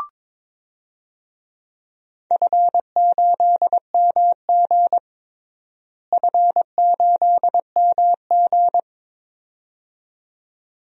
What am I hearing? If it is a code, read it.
Morse code, heard as a keyed tone, about 21 wpm, 700 Hz. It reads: F8MG F8MG
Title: DXCC-100 Call Signs - 4 Characters - Repeated 2x - 22wpm